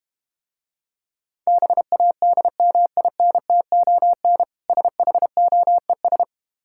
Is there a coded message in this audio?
Yes: Morse code from 1.5 to 6.2 s, heard as a keyed tone, about 32 wpm, 700 Hz. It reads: BADMINTON SHOES